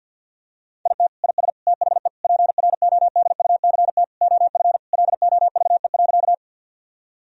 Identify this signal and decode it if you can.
Morse code, heard as a keyed tone, about 50 wpm, 700 Hz. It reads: IT IS THE PRODUCT OF LOVE.
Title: Sentences from Top 1000 Words - Review Part 2 - 50wpm